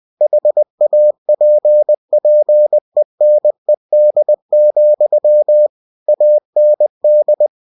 Morse code, heard as a keyed tone, about 20 wpm, 600 Hz. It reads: HAPPENED, AND
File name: cw-books / winnie-the-pooh / winnie-20wpm-page-0032